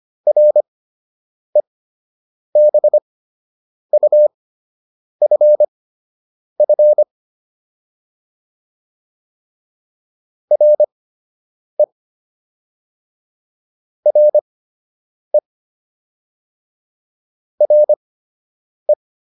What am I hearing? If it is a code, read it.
Morse code, heard as a keyed tone, about 25 wpm, spaced as for 8 wpm, 600 Hz. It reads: REBUFF RE RE RE